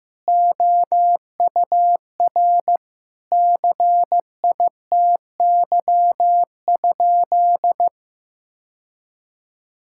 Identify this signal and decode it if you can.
Morse code, heard as a keyed tone, about 15 wpm, 700 Hz. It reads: OUR CITY?
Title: Sentences from Top 1000 Words - Review Part 5 - 15wpm